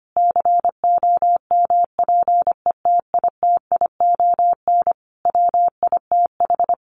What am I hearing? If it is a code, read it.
Morse code, heard as a keyed tone, about 25 wpm, 700 Hz. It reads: COMPETITION WITH
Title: Sentences from Top 1000 Words 25wpm